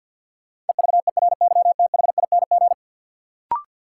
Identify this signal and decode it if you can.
Morse code, heard as a keyed tone, about 47 wpm, 700 Hz. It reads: EVERYTHING
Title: Sentences from Top 700 Words - New Word Drill - 50wpm